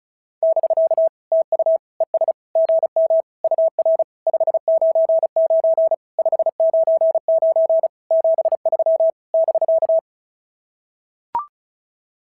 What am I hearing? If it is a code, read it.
Morse code, heard as a keyed tone, about 36 wpm, 690 Hz. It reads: <BK> TU ES GM UR 599 599 73 <BK>
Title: SOTA QSO Practice 35wpm